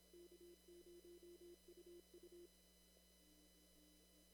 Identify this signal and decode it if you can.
Morse code, heard as a keyed tone, about 27 wpm, 340 Hz. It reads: K0UU
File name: CW / kt5j wpx cw-op-k5pi